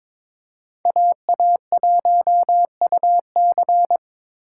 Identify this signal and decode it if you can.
Morse code, heard as a keyed tone, about 22 wpm, 700 Hz. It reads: AA1UC
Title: K1USN SST QSO Practice - 22wpm